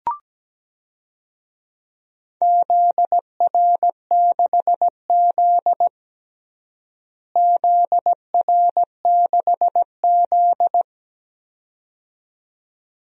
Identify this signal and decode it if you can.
Morse code, heard as a keyed tone, about 17 wpm, 700 Hz. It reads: ZR6Z ZR6Z
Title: DXCC-100 Call Signs - 4 Characters - Repeated 2x - 17wpm